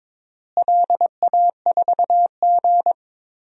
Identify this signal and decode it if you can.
Morse code, heard as a keyed tone, about 22 wpm, 700 Hz. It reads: LA4G
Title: DXCC-100 Call Signs - 4 Characters - Repeated 2x - 22wpm